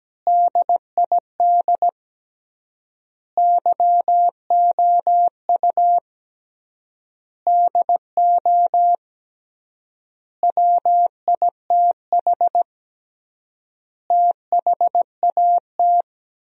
Morse code, heard as a keyed tone, about 17 wpm, 700 Hz. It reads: DID YOU DO WITH THAT
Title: 3x Word-Spacing - Sentences from Top 300 Words - 17wpm